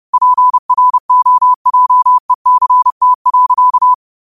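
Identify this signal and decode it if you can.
Morse code, heard as a keyed tone, about 30 wpm, 1000 Hz. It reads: PROJECT.
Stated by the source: Morse code quotes of the day (QOTD) sent at 30 words per minute.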